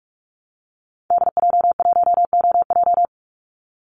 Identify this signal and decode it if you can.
Morse code, heard as a keyed tone, about 44 wpm, 700 Hz. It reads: DJ1OJ
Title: Call Signs Repeated 2x - 45wpm